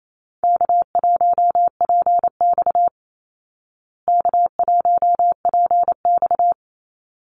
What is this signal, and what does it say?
Morse code, heard as a keyed tone, about 28 wpm, 700 Hz. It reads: K1PX K1PX
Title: Easy Call Signs Repeated 2x - 28wpm